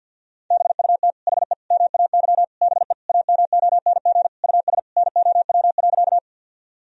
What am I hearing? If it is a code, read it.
Morse code, heard as a keyed tone, about 50 wpm, 700 Hz. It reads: BUT HE MAY BE AMONG US NOW.
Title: Sentences from Top 1000 Words - Review Part 3 - 50wpm